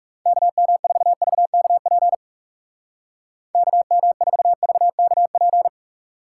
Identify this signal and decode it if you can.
Morse code, heard as a keyed tone, about 36 wpm, 700 Hz. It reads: KM4VKP KM4VKP
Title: Speed Racing - USA 2x3 Call Signs 30wpm